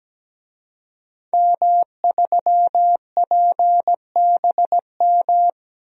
Morse code, heard as a keyed tone, about 17 wpm, 700 Hz. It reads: M3PBM
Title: DXCC-100 Call Signs - 5 Characters - Repeated 2x - 17wpm